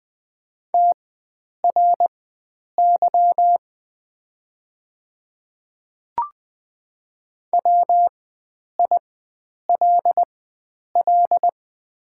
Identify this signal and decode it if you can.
Morse code, heard as a keyed tone, about 20 wpm, spaced as for 9 wpm, 700 Hz. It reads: TRY WILL
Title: 4x Character Spacing - Top 300 Words - 20wpm